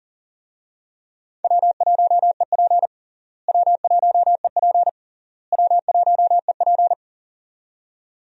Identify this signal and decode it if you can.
Morse code, heard as a keyed tone, about 37 wpm, 700 Hz. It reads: W1EP W1EP W1EP
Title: USA Call Signs - 4 Characters - Repeated 3x - 40wpm